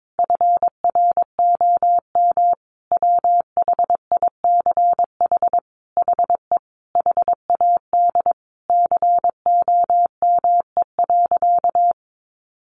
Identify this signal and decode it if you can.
Morse code, heard as a keyed tone, about 22 wpm, 700 Hz. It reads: FROM WHICH HE HAD COME.
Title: Sentences from Top 1500 Words 22wpm